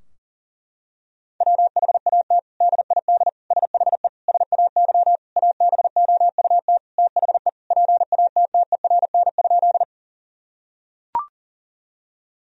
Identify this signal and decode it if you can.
Morse code, heard as a keyed tone, about 40 wpm, 700 Hz. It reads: WHAT DID SHE SAY ABOUT THE PATTERN?
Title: Sentences from Top 1000 Words - Review Part 2 - 40wpm